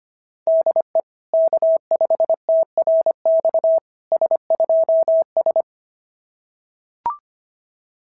Morse code, heard as a keyed tone, about 25 wpm, 660 Hz. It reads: DE K5TRX S2S
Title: SOTA QSO Practice 25wpm